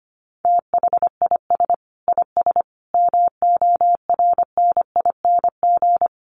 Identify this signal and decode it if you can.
Morse code, heard as a keyed tone, about 25 wpm, 700 Hz. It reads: THIS IS MORNING
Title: Sets of 5 Words Encore2 25wpm